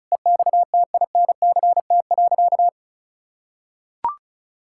Morse code, heard as a keyed tone, about 34 wpm, 700 Hz. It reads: EXTINCT.
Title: Pangram Challenge v2 - 35wpm